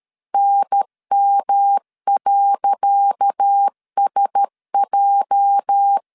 Morse code, heard as a keyed tone, about 13 wpm, 800 Hz. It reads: NM.SJ
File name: LEZIONI / 10 SIGNS / 10C DOT 12wpm 2451